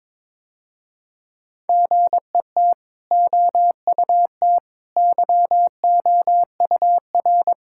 Morse code, heard as a keyed tone, about 21 wpm, 700 Hz. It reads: GET OUT YOUR
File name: practice / 22wpm / Easy Sentences with Hints - 22wpm